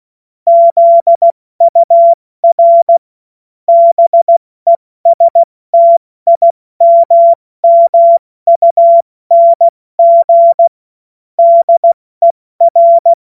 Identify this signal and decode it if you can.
Morse code, heard as a keyed tone, about 16 wpm, spaced as for 14 wpm, 680 Hz. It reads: ZUR BESTIMMUNG DER
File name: Morsekurs / Witze / Witz 10.Oktober 2023 70 F80